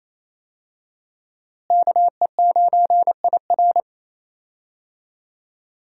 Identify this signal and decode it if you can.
Morse code, heard as a keyed tone, about 28 wpm, 700 Hz. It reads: KE9IR
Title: USA Call Signs - 5 Characters - Repeated 2x - 28wpm